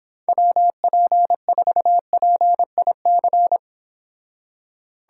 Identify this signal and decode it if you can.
Morse code, heard as a keyed tone, about 29 wpm, 700 Hz. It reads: WP4PIC